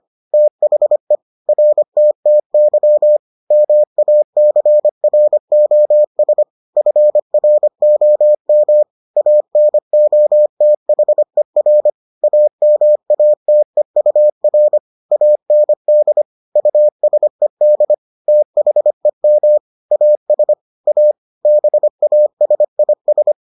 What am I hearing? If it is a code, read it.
Morse code, heard as a keyed tone, about 25 wpm, 600 Hz. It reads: THE RTTY MACROS FROM ANOTHER AMATEUR AND USED THEM AS A BASIS